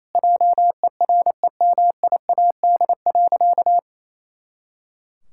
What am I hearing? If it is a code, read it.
Morse code, heard as a keyed tone, about 27 wpm, 700 Hz. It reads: JEREMIAD.
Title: Pangram Challenge v2 - 28wpm